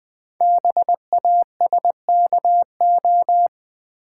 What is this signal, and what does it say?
Morse code, heard as a keyed tone, about 20 wpm, 700 Hz. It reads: BASKO
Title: Code Groups - 5 Letters - 20wpm